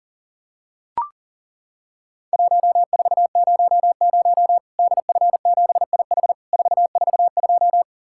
Morse code, heard as a keyed tone, about 39 wpm, 700 Hz. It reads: 1400 DF7IH 442
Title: Worked All Europe DX Contest - QTC Exchange - 40wpm